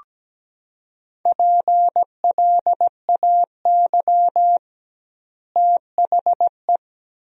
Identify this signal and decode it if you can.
Morse code, heard as a keyed tone, about 17 wpm, 700 Hz. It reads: PLAY THE
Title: 2x Word-Spacing - Sentences from Top 500 Words - 17wpm